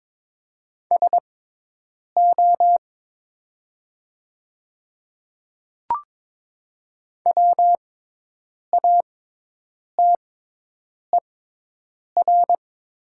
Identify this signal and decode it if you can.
Morse code, heard as a keyed tone, about 21 wpm, spaced as for 8 wpm, 700 Hz. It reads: SO WATER
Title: 6x Character Spacing - Top 100 Words - 22wpm